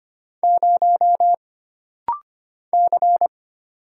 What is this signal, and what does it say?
Morse code, heard as a keyed tone, about 25 wpm, 700 Hz. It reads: 0 C
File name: Single Character (Letter:Number:Punctuation:Prosign) - Rapid-Fire - 25wpm